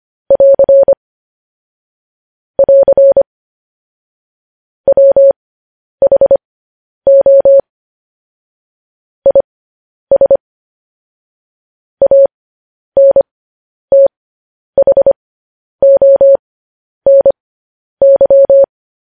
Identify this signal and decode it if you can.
Morse code, heard as a keyed tone, about 25 wpm, spaced as for 10 wpm, 550 Hz. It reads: <AR> <AR> WHO IS ANTHONY
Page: news headlines in morse code at 10 wpm